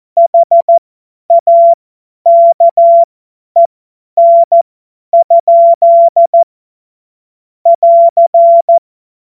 Morse code, heard as a keyed tone, about 14 wpm, spaced as for 10 wpm, 680 Hz. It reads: HAKEN? <AR>
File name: Morsekurs / Witze / Witz 27 Februar 2024 50 F70